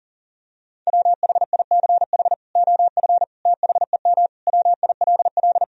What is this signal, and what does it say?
Morse code, heard as a keyed tone, about 40 wpm, 700 Hz. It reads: WHICH OF THEM WILL
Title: Sentences from Top 1000 Words - Review Part 3 - 40wpm